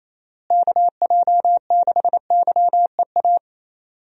Morse code, heard as a keyed tone, about 27 wpm, 700 Hz. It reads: KJ6YEA
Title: USA Call Signs - 6 Characters - Repeated 2x - 28wpm